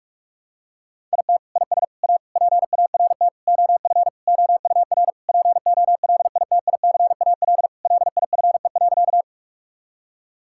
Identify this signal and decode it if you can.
Morse code, heard as a keyed tone, about 46 wpm, 700 Hz. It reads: IT IS A PART OF OUR POLITICAL LIFE.
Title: Sentences from Top 1000 Words - Review Part 5 - 45wpm